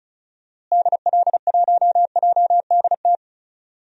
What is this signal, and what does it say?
Morse code, heard as a keyed tone, about 36 wpm, 700 Hz. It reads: DL1JDT
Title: DXCC-100 Call Signs - 6 Characters - Repeated 3x - 35wpm